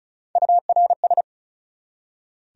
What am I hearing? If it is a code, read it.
Morse code, heard as a keyed tone, about 48 wpm, spaced as for 41 wpm, 700 Hz. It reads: URS
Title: Speed-Racing Abbreviations - 35wpm